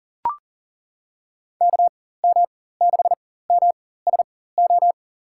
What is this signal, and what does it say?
Morse code, heard as a keyed tone, about 40 wpm, spaced as for 19 wpm, 700 Hz. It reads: KM6MSO
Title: USA Call Signs - 4x Character Spacing - 40wpm